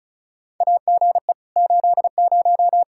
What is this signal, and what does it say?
Morse code, heard as a keyed tone, about 36 wpm, 700 Hz. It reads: AGE 80
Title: Practice QSO - Exchange 3 - More Info - 35wpm